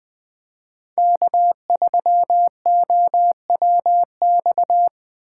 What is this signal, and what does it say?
Morse code, heard as a keyed tone, about 21 wpm, 700 Hz. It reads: K3OWX